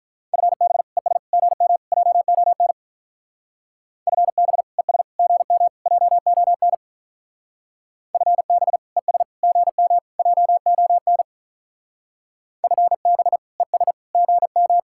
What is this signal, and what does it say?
Morse code, heard as a keyed tone, about 43 wpm, 700 Hz. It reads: FB ES GM JON FB ES GM JON FB ES GM JON FB ES GM
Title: Speed-Racing Common QSO Phrases Encore - 35wpm